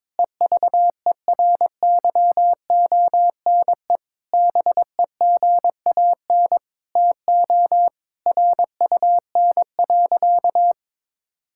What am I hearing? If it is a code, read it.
Morse code, heard as a keyed tone, about 21 wpm, 700 Hz. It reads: EVERYONE BEGAN TO RUN.
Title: Sentences from Top 1000 Words - Review Part 5 - 22wpm